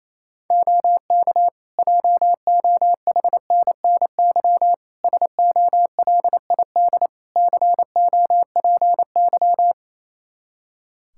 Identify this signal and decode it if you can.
Morse code, heard as a keyed tone, about 28 wpm, 700 Hz. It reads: OK JOHNNY SOLID COPY
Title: Practice QSO - Exchange 3 - More Info - 28wpm